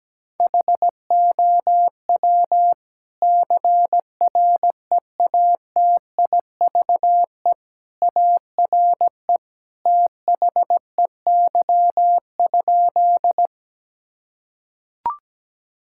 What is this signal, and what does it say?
Morse code, heard as a keyed tone, about 17 wpm, 700 Hz. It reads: HOW CREATIVE ARE THEY?
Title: Sentences from Top 1000 Words - Review Part 5 - 17wpm